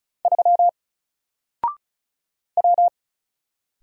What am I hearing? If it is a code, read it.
Morse code, heard as a keyed tone, about 34 wpm, 700 Hz. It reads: 3 W
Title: Single Character (Letter:Number:Punctuation:Prosign) Alt 35wpm